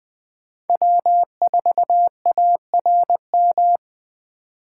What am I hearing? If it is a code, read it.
Morse code, heard as a keyed tone, about 21 wpm, 700 Hz. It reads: W4ARM
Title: USA Call Signs - 5 Characters - 3x Character Spacing - 20wpm